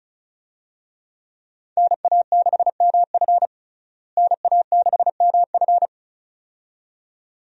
Morse code, heard as a keyed tone, about 34 wpm, 700 Hz. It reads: NA6MF NA6MF
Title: USA Call Signs - 5 Characters - Repeated 2x - 35wpm